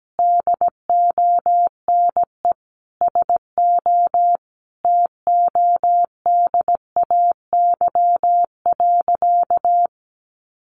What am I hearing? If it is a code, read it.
Morse code, heard as a keyed tone, about 17 wpm, 700 Hz. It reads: DONE SO TODAY.
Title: Sentences from Top 1500 Words 17wpm